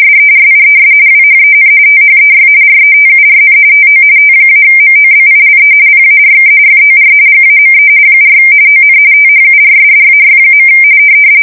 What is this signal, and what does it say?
RTTY, about 45.45 baud. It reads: SERIES WHEN THEY RETURN TO T-MOBILE PARK FOR GAME 4 ON THURSDAY.
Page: 170 Shift 2125/2295